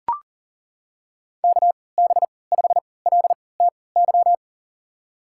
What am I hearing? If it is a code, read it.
Morse code, heard as a keyed tone, about 37 wpm, spaced as for 22 wpm, 700 Hz. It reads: KB5LTY
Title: USA Call Signs - 6 Characters - 3x Character Spacing - 40wpm